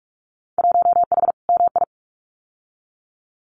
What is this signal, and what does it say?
Morse code, heard as a keyed tone, about 46 wpm, 700 Hz. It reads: 1H MI
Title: Winter Field Day Exchanges 45wpm